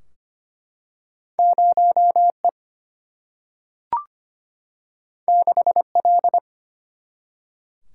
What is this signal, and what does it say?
Morse code, heard as a keyed tone, about 25 wpm, 700 Hz. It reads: 0E 6L
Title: Single-Letter Single-Number 25wpm